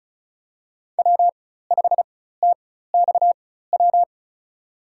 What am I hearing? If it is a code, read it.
Morse code, heard as a keyed tone, about 36 wpm, spaced as for 16 wpm, 700 Hz. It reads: W5TXW
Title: USA Call Signs - 5 Characters - 4x Character Spacing - 35wpm